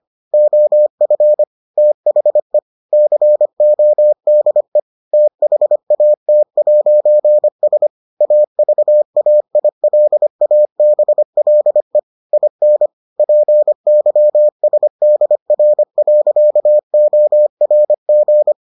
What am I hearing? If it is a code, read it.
Morse code, heard as a keyed tone, about 25 wpm, 600 Hz. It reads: OF THE CODE THAT'S AVAILABLE IN PYSDR.ORG